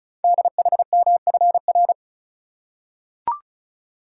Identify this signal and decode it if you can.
Morse code, heard as a keyed tone, about 36 wpm, 700 Hz. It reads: DHMFR
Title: Code Groups - 5 Letters - 35wpm